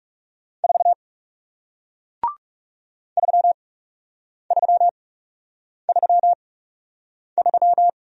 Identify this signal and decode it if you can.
Morse code, heard as a keyed tone, about 39 wpm, spaced as for 8 wpm, 700 Hz. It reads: 4 3333